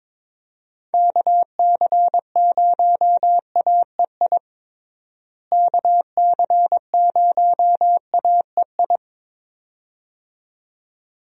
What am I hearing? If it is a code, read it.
Morse code, heard as a keyed tone, about 22 wpm, 700 Hz. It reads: KC0AEI KC0AEI
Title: USA Call Signs - 6 Characters - Repeated 2x - 22wpm